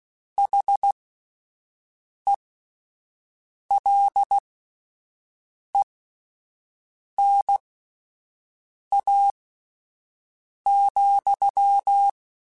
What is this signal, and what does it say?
Morse code, heard as a keyed tone, about 16 wpm, spaced as for 5 wpm, 800 Hz. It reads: HELENA,
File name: QSOs / QSOs-mp3s / WD0BGP 1@13